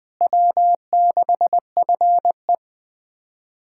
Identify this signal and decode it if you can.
Morse code, heard as a keyed tone, about 20 wpm, 700 Hz. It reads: W6FE
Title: USA Callsign DE Callsign - 4 Characters - 20wpm